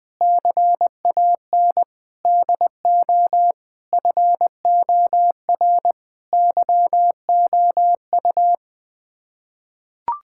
Morse code, heard as a keyed tone, about 20 wpm, 700 Hz. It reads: CAN DO FOR YOU
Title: Sets of 9 Words - 20wpm